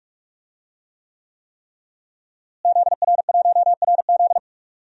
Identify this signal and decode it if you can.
Morse code, heard as a keyed tone, about 41 wpm, 700 Hz. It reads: ZR1RZ
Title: DXCC-100 Call Signs - 5 Characters - Repeated 3x - 45wpm